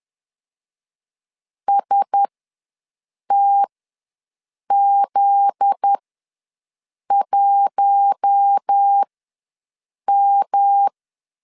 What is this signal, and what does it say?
Morse code, heard as a keyed tone, about 10 wpm, spaced as for 6 wpm, 800 Hz. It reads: STZ1M
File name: LEZIONI / 7 GF9Z / 7 GF9Z ws 10wpm 3919